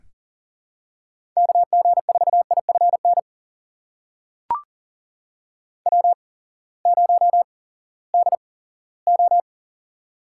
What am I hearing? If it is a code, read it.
Morse code, heard as a keyed tone, about 40 wpm, spaced as for 20 wpm, 700 Hz. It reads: KG4IFN W0DO
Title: USA Call Signs - 8x Character Spacing - 40wpm